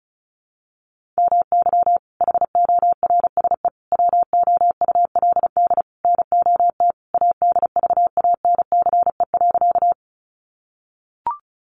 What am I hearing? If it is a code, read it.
Morse code, heard as a keyed tone, about 34 wpm, 700 Hz. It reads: MY HORSE WOULD NOT ADVANCE.
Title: Sentences from Top 1400 Words 35wpm